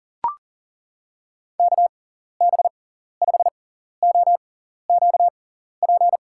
Morse code, heard as a keyed tone, about 40 wpm, spaced as for 14 wpm, 700 Hz. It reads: KB5OQP
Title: USA Call Signs - 6 Characters - 6x Character Spacing - 40wpm